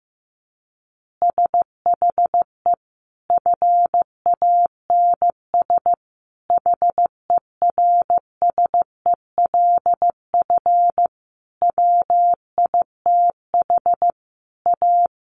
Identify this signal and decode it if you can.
Morse code, heard as a keyed tone, about 14 wpm, 700 Hz. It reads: SHE FANS HERSELF WITH A
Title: Additional Easy Phrases 15wpm